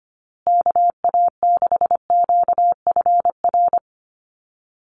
Morse code, heard as a keyed tone, about 25 wpm, 700 Hz. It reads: KA6QFR
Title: USA Call Signs Repeated 3x 25wpm